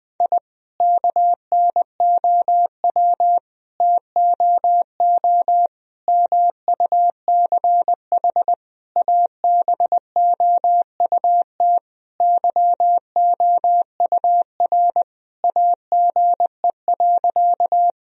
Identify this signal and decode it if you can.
Morse code, heard as a keyed tone, about 20 wpm, 700 Hz. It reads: I KNOW TOO MUCH ABOUT YOUR AGE.
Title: Sentences from Top 1000 Words - Review Part 3 - 20wpm